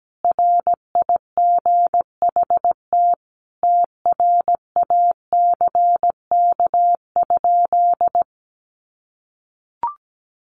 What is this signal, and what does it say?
Morse code, heard as a keyed tone, about 17 wpm, 700 Hz. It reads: RIGHT TRACK?
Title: Sentences from Top 1100 Words 17wpm